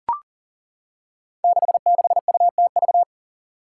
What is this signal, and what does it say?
Morse code, heard as a keyed tone, about 38 wpm, 700 Hz. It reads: 66UTV
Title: Five Challenging Letter-Number Combinations 40wpm